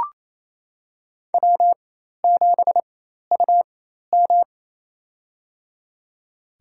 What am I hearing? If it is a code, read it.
Morse code, heard as a keyed tone, about 27 wpm, spaced as for 13 wpm, 700 Hz. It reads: W7UM
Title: USA Call Signs - 4x Character Spacing - 28wpm